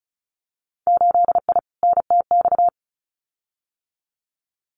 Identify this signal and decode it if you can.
Morse code, heard as a keyed tone, about 36 wpm, 700 Hz. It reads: 8I NTX
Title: Winter Field Day Exchanges Encore 35wpm